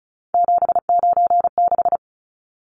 Morse code, heard as a keyed tone, about 34 wpm, 700 Hz. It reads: 796
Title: Numbers - 3-Digit 35wpm